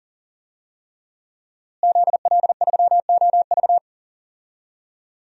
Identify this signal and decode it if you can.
Morse code, heard as a keyed tone, about 40 wpm, 700 Hz. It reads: ZL3OV
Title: DXCC-100 Call Signs - 5 Characters - Repeated 3x - 40wpm